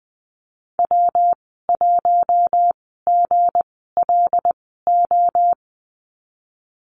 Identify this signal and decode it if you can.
Morse code, heard as a keyed tone, about 20 wpm, spaced as for 15 wpm, 700 Hz. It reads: W1GLO
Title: 186 Lesson 30g - Letter b - Call Signs 20wpm